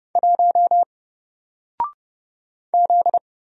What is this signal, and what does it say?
Morse code, heard as a keyed tone, about 31 wpm, 700 Hz. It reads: 1 Z
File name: Single-Letter-Number-30wpm